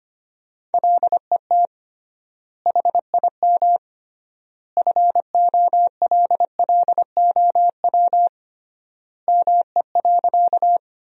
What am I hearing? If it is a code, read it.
Morse code, heard as a keyed tone, about 25 wpm, 700 Hz. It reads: LET HIM FOLLOW ME.
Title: 3x Word-Spacing - Sentences from Top 300 Words - 25wpm